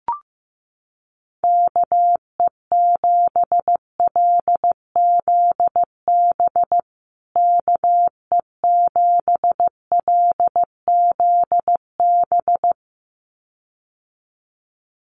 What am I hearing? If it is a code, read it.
Morse code, heard as a keyed tone, about 15 wpm, 700 Hz. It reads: KE7LZB KE7LZB
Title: USA Call Signs Repeated 2x 15wpm